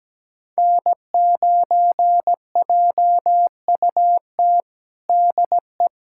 Morse code, heard as a keyed tone, about 17 wpm, 700 Hz. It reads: N9JUT DE
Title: USA Callsign DE Callsign - 5 Characters - 17wpm